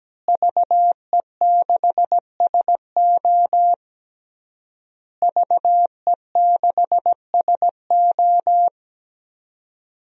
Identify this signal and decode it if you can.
Morse code, heard as a keyed tone, about 17 wpm, 700 Hz. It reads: VE6SO VE6SO
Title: DXCC-100 Call Signs - 5 Characters - Repeated 3x - 17wpm